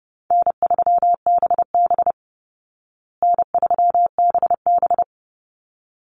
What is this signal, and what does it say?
Morse code, heard as a keyed tone, about 30 wpm, 700 Hz. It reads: N3BB N3BB
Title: Call Signs Repeated 3x - 30wpm